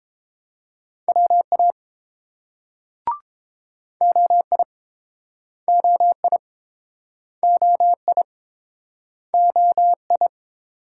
Morse code, heard as a keyed tone, about 31 wpm, 700 Hz. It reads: WA OI OI OI OI